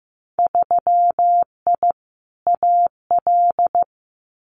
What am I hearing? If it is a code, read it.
Morse code, heard as a keyed tone, about 15 wpm, 700 Hz. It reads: 3I AL
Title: Winter Field Day Exchanges 15wpm